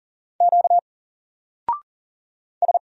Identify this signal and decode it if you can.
Morse code, heard as a keyed tone, about 40 wpm, 700 Hz. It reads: Q S
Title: Single Character (Letter:Number:Punctuation:Prosign) Alt 40wpm